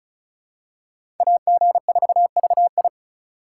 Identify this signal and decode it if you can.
Morse code, heard as a keyed tone, about 36 wpm, 700 Hz. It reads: AG4VI
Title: USA Call Signs - 5 Characters - 3x Character Spacing - 35wpm